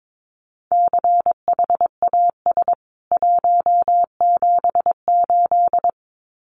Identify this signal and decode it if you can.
Morse code, heard as a keyed tone, about 22 wpm, 700 Hz. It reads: CHAS 178
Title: CWT Exchange 22wpm